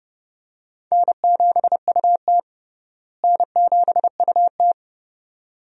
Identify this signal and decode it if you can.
Morse code, heard as a keyed tone, about 30 wpm, 700 Hz. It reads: N7UT N7UT
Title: USA Call Signs - 4 Characters - Repeated 2x - 30wpm